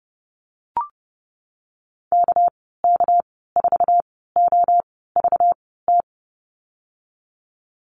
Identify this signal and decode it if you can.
Morse code, heard as a keyed tone, about 30 wpm, spaced as for 17 wpm, 700 Hz. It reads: KK4OVT
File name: Lesson 37g - Letter x - Call Signs 30wpm